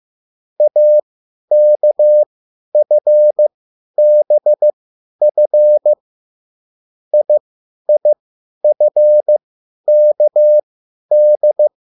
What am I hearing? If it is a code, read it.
Morse code, heard as a keyed tone, about 15 wpm, spaced as for 10 wpm, 600 Hz. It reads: AKFBF IIFKD
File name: QSL / CW lesson